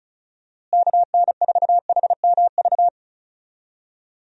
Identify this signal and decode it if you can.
Morse code, heard as a keyed tone, about 36 wpm, 700 Hz. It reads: KN4HMV
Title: USA Call Signs - 6 Characters - Repeated 3x - 35wpm